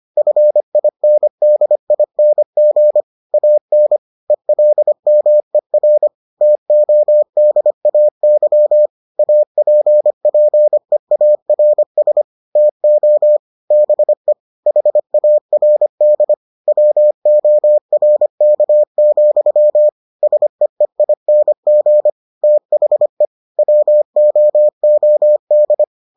Morse code, heard as a keyed tone, about 25 wpm, 600 Hz. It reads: FINDING AN ELMER TODAY APPEARS TO BE HARD WORK, SEEING THE WOOD